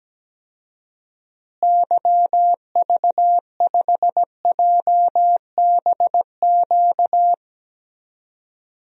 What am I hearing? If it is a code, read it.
Morse code, heard as a keyed tone, about 17 wpm, 700 Hz. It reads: YV5JBQ
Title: DXCC-100 Call Signs - 6 Characters - Repeated 2x - 17wpm